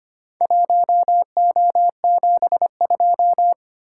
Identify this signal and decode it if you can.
Morse code, heard as a keyed tone, about 25 wpm, 700 Hz. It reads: 1O72
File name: practice / 25wpm / Four Letter-Number Combinations - 25wpm